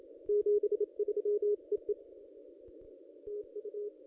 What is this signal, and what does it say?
Morse code, heard as a keyed tone, about 27 wpm, 440 Hz. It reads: 73 EE TU
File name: contests / ss-cw-k5tr-op-n5rz -1400